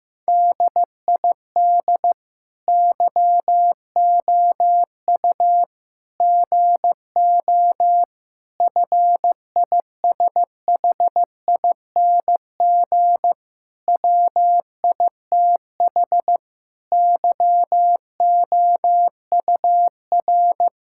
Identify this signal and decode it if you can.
Morse code, heard as a keyed tone, about 15 wpm, 700 Hz. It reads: DID YOU GO FISHING WITH YOUR
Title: Sentences from Top 1000 Words - Review Part 5 - 15wpm